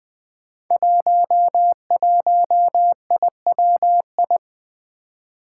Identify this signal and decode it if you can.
Morse code, heard as a keyed tone, about 20 wpm, 700 Hz. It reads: 11IWI
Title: Five Challenging Letter-Number Combinations 20wpm